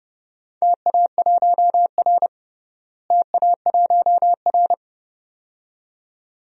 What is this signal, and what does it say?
Morse code, heard as a keyed tone, about 30 wpm, 700 Hz. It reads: TA1R TA1R
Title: DXCC-100 Call Signs - 4 Characters - Repeated 2x - 30wpm